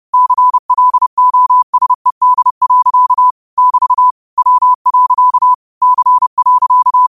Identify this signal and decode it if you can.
Morse code, heard as a keyed tone, about 30 wpm, 1000 Hz. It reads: CLOSED. <BT> W. C.
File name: qotdfiles / 30wpm